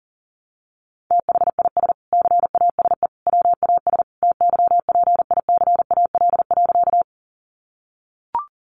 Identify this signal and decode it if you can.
Morse code, heard as a keyed tone, about 40 wpm, 700 Hz. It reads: THIS CASE WAS TYPICAL.
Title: Sentences from Top 1100 Words 40wpm